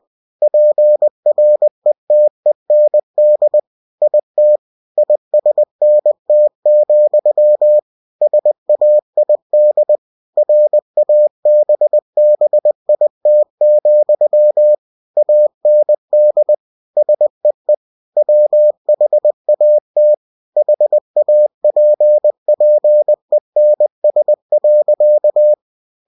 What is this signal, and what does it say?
Morse code, heard as a keyed tone, about 20 wpm, 600 Hz. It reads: PRETEND IT ISNT, SAID RABBIT, AND SEE WHAT HAPPENS.